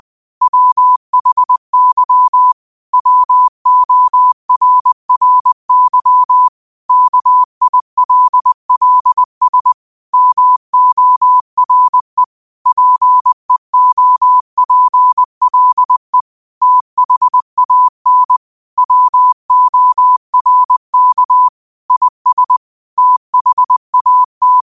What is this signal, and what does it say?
Morse code, heard as a keyed tone, about 20 wpm, 1000 Hz. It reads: WHY WORRY KILLS MORE PEOPLE THAN WORK IS THAT